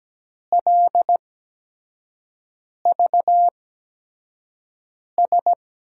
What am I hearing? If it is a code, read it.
Morse code, heard as a keyed tone, about 17 wpm, spaced as for 5 wpm, 700 Hz. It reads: LVS
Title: USA Call Signs - 6 Characters - 8x Character Spacing - 17wpm